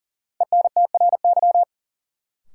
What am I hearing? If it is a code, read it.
Morse code, heard as a keyed tone, about 40 wpm, 700 Hz. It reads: ENTRY
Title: Sentences from Top 1500 Words - New Word Drill - 40wpm